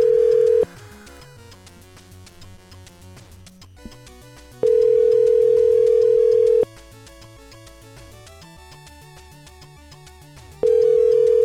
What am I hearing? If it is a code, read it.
Ringback tone (North America), 440 + 480 Hz.